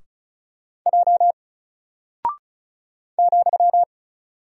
Morse code, heard as a keyed tone, about 36 wpm, 700 Hz. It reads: J ,
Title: Single Character (Letter:Number:Punctuation:Prosign) 35wpm